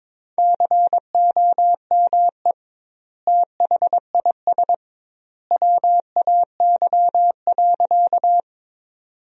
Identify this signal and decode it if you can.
Morse code, heard as a keyed tone, about 22 wpm, 700 Hz. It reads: COME THIS WAY.
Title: 2x Word-Spacing - Sentences from Top 100 Words - 22wpm